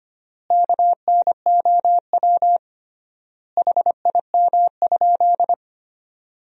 Morse code, heard as a keyed tone, about 25 wpm, 700 Hz. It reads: KNOW HIM?
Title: 3x Word-Spacing - Sentences from Top 100 Words - 25wpm